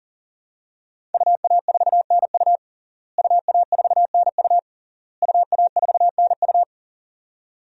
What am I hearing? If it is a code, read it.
Morse code, heard as a keyed tone, about 37 wpm, 700 Hz. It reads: UA4NU UA4NU UA4NU
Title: DXCC-100 Call Signs - 5 Characters - Repeated 3x - 40wpm